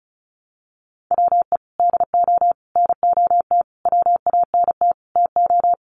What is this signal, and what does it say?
Morse code, heard as a keyed tone, about 34 wpm, 700 Hz. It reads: WE DO NOT WANT TO
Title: Sets of 5 Words 35wpm